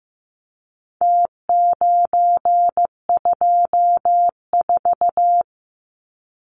Morse code, heard as a keyed tone, about 15 wpm, 700 Hz. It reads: T924
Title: Numbers - Contest Style - 4-Digit 15wpm